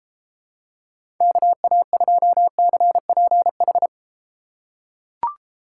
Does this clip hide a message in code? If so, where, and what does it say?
Yes: Morse code from 1.2 to 3.9 s, heard as a keyed tone, about 30 wpm, 700 Hz. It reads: KA2CPH